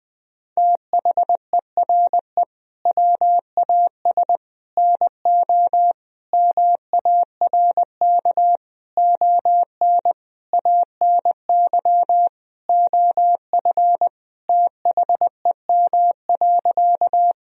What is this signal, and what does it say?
Morse code, heard as a keyed tone, about 20 wpm, 700 Hz. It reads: THERE WAS NO MARK ON ANY OF THEM.
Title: Sentences from Top 1000 Words - Review Part 1 - 20wpm